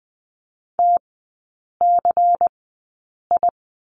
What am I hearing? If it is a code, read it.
Morse code, heard as a keyed tone, about 20 wpm, spaced as for 8 wpm, 700 Hz. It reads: TCI
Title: 133 Lesson 22e - Question Mark - Three Characters 20wpm